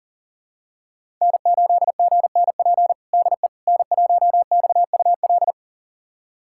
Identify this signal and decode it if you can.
Morse code, heard as a keyed tone, about 40 wpm, 700 Hz. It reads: N8GNP DE N1XUL
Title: USA Callsign DE Callsign - 5 Characters - 40wpm